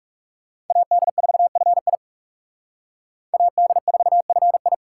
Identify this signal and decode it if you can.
Morse code, heard as a keyed tone, about 38 wpm, 700 Hz. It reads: AD4FI AD4FI